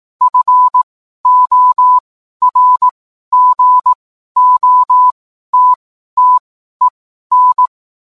Morse code, heard as a keyed tone, about 18 wpm, spaced as for 12 wpm, 1000 Hz. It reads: FORGOTTEN